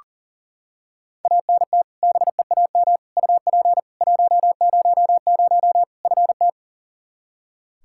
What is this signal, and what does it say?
Morse code, heard as a keyed tone, about 40 wpm, 700 Hz. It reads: ANT BEAM UP 100 FT
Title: K1USN SST QSO Practice 40wpm